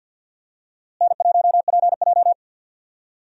Morse code, heard as a keyed tone, about 47 wpm, 700 Hz. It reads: N1PJ
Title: USA Call Signs - 4 Characters - Repeated 2x - 50wpm